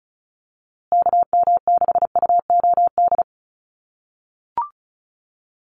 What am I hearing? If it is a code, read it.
Morse code, heard as a keyed tone, about 34 wpm, 700 Hz. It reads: KM6UOD
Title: USA Call Signs Repeated 2x 35wpm